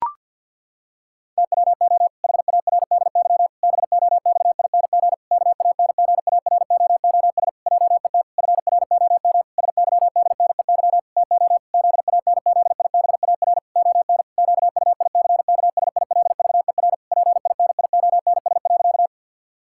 Morse code, heard as a keyed tone, about 49 wpm, 700 Hz. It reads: TWO HARDY BOXING KANGAROOS JET FROM SYDNEY TO ZANZIBAR ON QUICKSILVER PINIONS.
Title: Pangram Challenge v2 - 50wpm